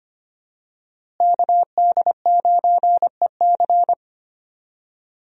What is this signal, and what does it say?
Morse code, heard as a keyed tone, about 25 wpm, 700 Hz. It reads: KD9EC
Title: USA Call Signs - 5 Characters - Repeated 2x - 25wpm